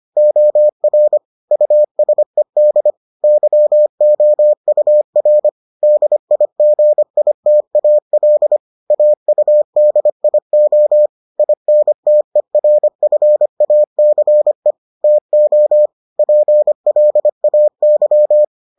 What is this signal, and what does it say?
Morse code, heard as a keyed tone, about 25 wpm, 600 Hz. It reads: OR USED YOUR DIGITAL AUDIO INTERFACE TO PLAY